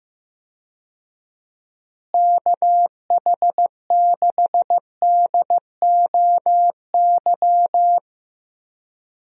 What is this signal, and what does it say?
Morse code, heard as a keyed tone, about 15 wpm, 700 Hz. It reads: KH6DOY
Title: DXCC-100 Call Signs - 6 Characters - Repeated 2x - 15wpm